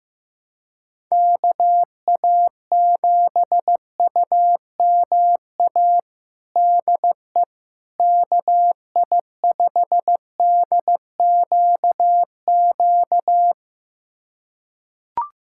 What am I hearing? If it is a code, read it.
Morse code, heard as a keyed tone, about 15 wpm, 700 Hz. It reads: KA7UMA DE KI5DQQ
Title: USA Callsign DE Callsign - 6 Characters - 15wpm